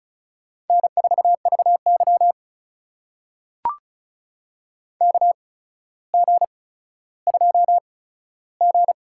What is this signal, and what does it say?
Morse code, heard as a keyed tone, about 36 wpm, spaced as for 15 wpm, 700 Hz. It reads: N4VY KG2G
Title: USA Call Signs - 4 Characters - 8x Character Spacing - 35wpm